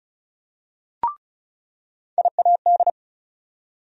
Morse code, heard as a keyed tone, about 36 wpm, 700 Hz. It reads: IAD
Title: USA Call Sign Postfixes - 3 Letters 35wpm